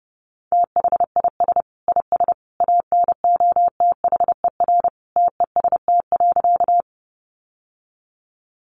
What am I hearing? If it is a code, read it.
Morse code, heard as a keyed tone, about 28 wpm, 700 Hz. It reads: THIS IS ANOTHER TEST.
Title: Sentences from Top 600 Words 30wpm